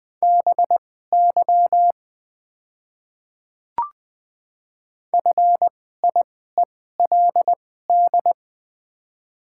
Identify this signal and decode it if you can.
Morse code, heard as a keyed tone, about 21 wpm, spaced as for 15 wpm, 700 Hz. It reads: BY FIELD
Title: 2x Character Spacing - Top 500 Words - 20wpm